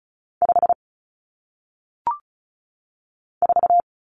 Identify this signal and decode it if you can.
Morse code, heard as a keyed tone, about 35 wpm, 700 Hz. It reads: 5 4
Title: Common Cut Numbers - 1-Digit 35wpm